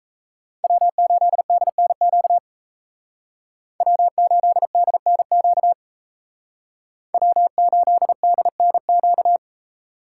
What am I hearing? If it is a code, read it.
Morse code, heard as a keyed tone, about 36 wpm, 700 Hz. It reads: W8DNQ W8DNQ W8DNQ